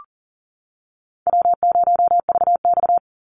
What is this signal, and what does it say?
Morse code, heard as a keyed tone, about 40 wpm, 700 Hz. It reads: W0VX
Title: Call Signs Repeated 3x - 40wpm